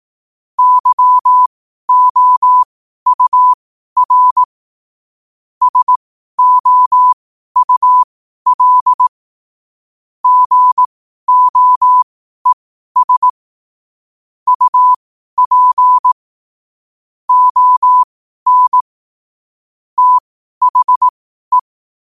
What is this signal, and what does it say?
Morse code, heard as a keyed tone, about 18 wpm, spaced as for 12 wpm, 1000 Hz. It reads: YOUR SOUL GOES UP ON THE
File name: qotdfiles / 12wpm